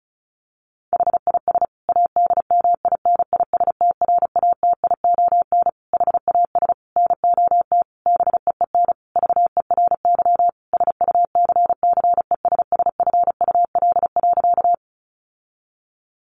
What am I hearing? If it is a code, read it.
Morse code, heard as a keyed tone, about 34 wpm, 700 Hz. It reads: HIS ADMINISTRATION HAS NOT BEEN VERY SUCCESSFUL.
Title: Sentences from Top 1300 Words 35wpm